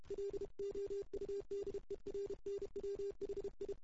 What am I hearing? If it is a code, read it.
Morse code, heard as a keyed tone, about 33 wpm, 410 Hz. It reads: LOUDER NW HI
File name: CW / 2008-ss-cw-k5tr-op-n5rz -04:00